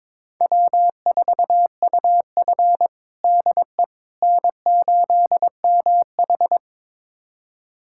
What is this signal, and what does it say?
Morse code, heard as a keyed tone, about 22 wpm, 700 Hz. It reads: W4UF DE N8MH
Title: USA Callsign DE Callsign - 4 Characters - 22wpm